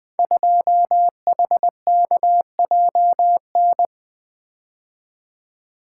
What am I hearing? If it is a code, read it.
Morse code, heard as a keyed tone, about 20 wpm, 700 Hz. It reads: 2HKJN
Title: Code Groups - 5 Letter:Numbers - 20wpm